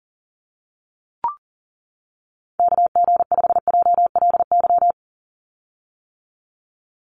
Morse code, heard as a keyed tone, about 40 wpm, 700 Hz. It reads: KG5JLY
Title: USA Call Signs 40wpm